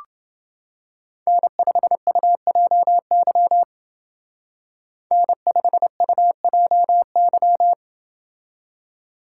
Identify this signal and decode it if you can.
Morse code, heard as a keyed tone, about 29 wpm, 700 Hz. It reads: N5UJY N5UJY